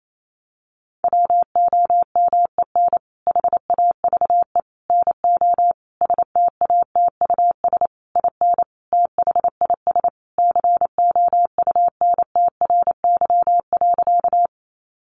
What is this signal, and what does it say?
Morse code, heard as a keyed tone, about 28 wpm, 700 Hz. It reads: WOMEN HAVE NO STATUS IN THIS COUNTRY.
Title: Sentences from Top 1400 Words 28wpm